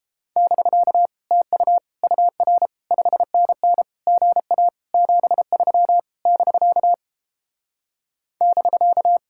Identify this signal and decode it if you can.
Morse code, heard as a keyed tone, about 29 wpm, 700 Hz. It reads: <BK> TU UR 5NN GA 73 <BK> <BK>